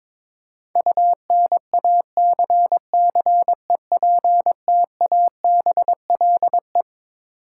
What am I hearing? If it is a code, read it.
Morse code, heard as a keyed tone, about 21 wpm, 700 Hz. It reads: UNACCEPTABLE
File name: Twelve Letter Words - 22wpm